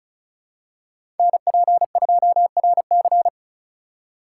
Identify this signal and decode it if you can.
Morse code, heard as a keyed tone, about 36 wpm, 700 Hz. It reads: NP2RC
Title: USA Call Signs - 5 Characters - 4x Character Spacing - 35wpm